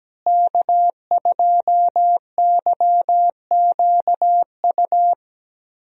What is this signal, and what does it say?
Morse code, heard as a keyed tone, about 17 wpm, 700 Hz. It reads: K2YQU
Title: USA Call Signs - 5 Characters - Repeated 3x - 17wpm